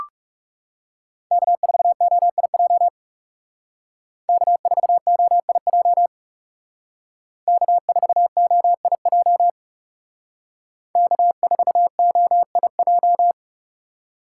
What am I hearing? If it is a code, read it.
Morse code, heard as a keyed tone, about 37 wpm, 700 Hz. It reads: K4OIJ K4OIJ K4OIJ K4OIJ
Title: Speed Racing - USA Call Signs 30wpm